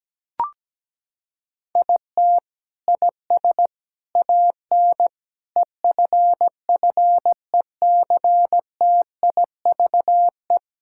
Morse code, heard as a keyed tone, about 17 wpm, 700 Hz. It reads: IT IS AN EFFECTIVE
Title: Sentences from Top 1000 Words - Review Part 6 - 17wpm